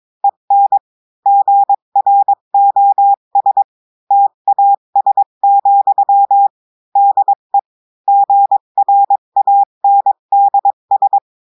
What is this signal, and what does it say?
Morse code, heard as a keyed tone, about 22 wpm, 800 Hz. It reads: EN GROS TAS, DE GRANDS
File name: cours-du-mercredi / audios / cours-853-du-2-mai-2024-etxte chiffre et texte clair-a-20-wpm